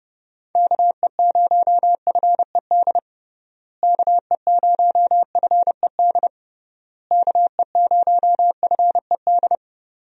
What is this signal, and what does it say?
Morse code, heard as a keyed tone, about 30 wpm, 700 Hz. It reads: KE0FED KE0FED KE0FED
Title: USA Call Signs - 6 Characters - Repeated 3x - 30wpm